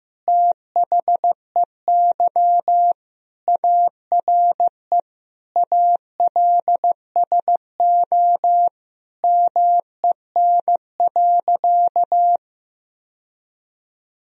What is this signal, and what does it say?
Morse code, heard as a keyed tone, about 16 wpm, 700 Hz. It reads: THEY ARE ALSO MEN.
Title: Sentences from Top 1000 Words - Review Part 1 - 15wpm